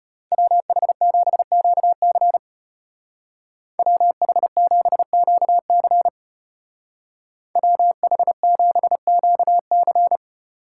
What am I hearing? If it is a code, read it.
Morse code, heard as a keyed tone, about 34 wpm, 700 Hz. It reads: WH7QC WH7QC WH7QC